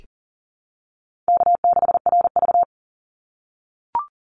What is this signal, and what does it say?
Morse code, heard as a keyed tone, about 40 wpm, 700 Hz. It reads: K6RV
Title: Easy Call Signs - 40wpm